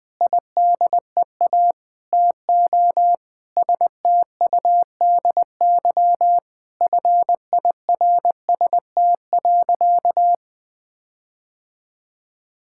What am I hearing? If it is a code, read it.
Morse code, heard as a keyed tone, about 20 wpm, 700 Hz. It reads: IDEA TO STUDY FIRST.
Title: Sentences with 7-Word Phrases - Spoken Beginning - 20wpm